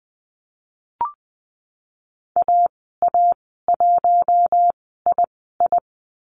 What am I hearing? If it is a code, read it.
Morse code, heard as a keyed tone, about 20 wpm, spaced as for 15 wpm, 700 Hz. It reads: AA1II
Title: 048 Lesson 9g - Number 4 - Call Signs 20wpm